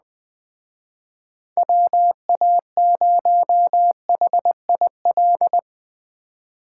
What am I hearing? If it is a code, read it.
Morse code, heard as a keyed tone, about 20 wpm, 700 Hz. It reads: WA0HIL